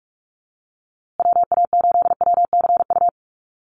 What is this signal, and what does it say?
Morse code, heard as a keyed tone, about 44 wpm, 700 Hz. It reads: WA8WCU
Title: USA Call Signs Repeated 3x 45wpm